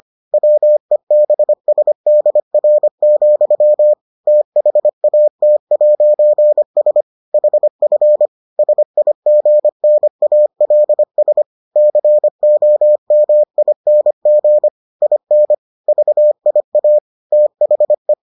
Morse code, heard as a keyed tone, about 25 wpm, 600 Hz. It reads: WEBSDR, THAT'S HF SIGNALS COMING IN VIA THE